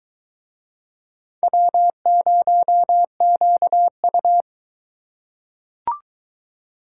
Morse code, heard as a keyed tone, about 22 wpm, 700 Hz. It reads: W0QU